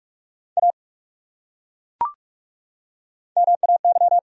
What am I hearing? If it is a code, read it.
Morse code, heard as a keyed tone, about 43 wpm, 700 Hz. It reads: A MAY
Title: Top 100 Words 45wpm